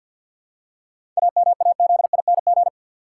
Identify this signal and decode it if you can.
Morse code, heard as a keyed tone, about 52 wpm, 700 Hz. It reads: AMAZING
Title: Seven Letter Words - 50wpm